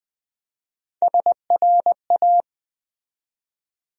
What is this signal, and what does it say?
Morse code, heard as a keyed tone, about 20 wpm, 700 Hz. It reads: SRA
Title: USA Call Sign Postfixes - 3 Letters 20wpm